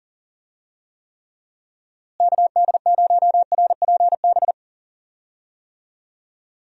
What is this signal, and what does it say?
Morse code, heard as a keyed tone, about 40 wpm, 700 Hz. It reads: KD0RPB
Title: USA Call Signs - 6 Characters - Repeated 3x - 40wpm